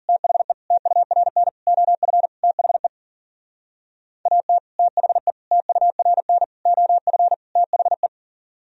Morse code, heard as a keyed tone, about 49 wpm, 700 Hz. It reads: THE TURN OF THE AT THE TURN OF THE
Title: Speed-Racing - Sets of 5 Words 40wpm